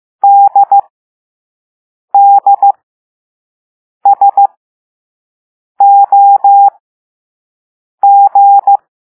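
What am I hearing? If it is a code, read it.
Morse code, heard as a keyed tone, about 15 wpm, spaced as for 5 wpm, 800 Hz. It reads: DDSOG
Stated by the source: Morse code lesson 5 drill of random five-letter groups using all letters learned so far.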